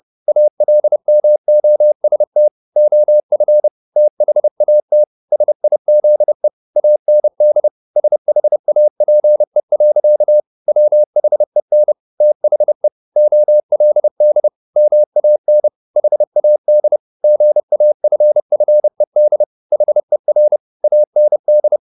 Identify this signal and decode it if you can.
Morse code, heard as a keyed tone, about 30 wpm, 600 Hz. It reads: ALMOST OF THAT SIZE AND SHAPE. WHEN THE OLD MAN HAD GAFFED HER AND
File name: the old man and the sea / the old man and the sea 1 30wpm